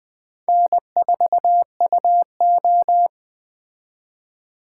Morse code, heard as a keyed tone, about 21 wpm, 700 Hz. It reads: N4UO
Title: Speed Racing - USA 1x2 Call Signs 17wpm